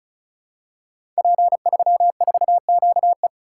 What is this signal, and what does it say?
Morse code, heard as a keyed tone, about 36 wpm, 700 Hz. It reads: P34QE
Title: Code Groups - 5 Letter:Numbers - 35wpm